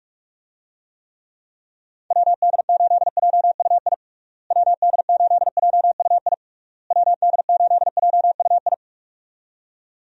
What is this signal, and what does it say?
Morse code, heard as a keyed tone, about 41 wpm, 700 Hz. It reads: WD8JUI WD8JUI WD8JUI
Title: USA Call Signs - 6 Characters - Repeated 3x - 45wpm